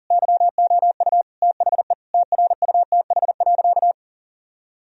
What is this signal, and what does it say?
Morse code, heard as a keyed tone, about 40 wpm, 700 Hz. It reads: YOU THE TRUTH.
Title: Sentences from Top 1000 Words - Review Part 2 - 40wpm